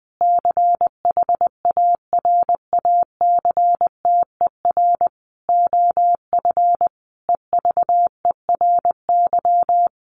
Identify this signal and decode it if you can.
Morse code, heard as a keyed tone, about 19 wpm, 700 Hz. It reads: CHARACTER OF EVERY
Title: Sentences from Top 1400 Words 20wpm